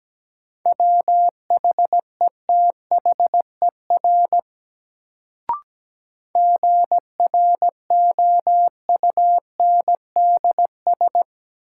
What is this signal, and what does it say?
Morse code, heard as a keyed tone, about 17 wpm, 700 Hz. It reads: WHETHER GROUNDS
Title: Sentences from Top 1300 Words - New Word Drill Alt - 17wpm